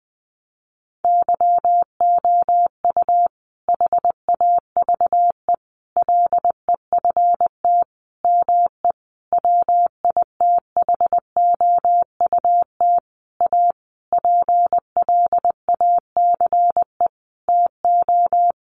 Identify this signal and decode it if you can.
Morse code, heard as a keyed tone, about 20 wpm, 700 Hz. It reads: YOU HAVE LEFT ME WITHOUT A PLACE TO
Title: Sentences from Top 600 Words 20wpm